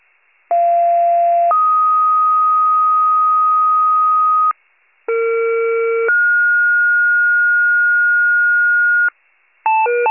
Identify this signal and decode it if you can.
Two-tone page, heard as a one-second tone, then a three-second tone, about 690 then 1250 Hz, about 460 then 1430 Hz, with noise. It is heard from 0.5 to 9.1 s.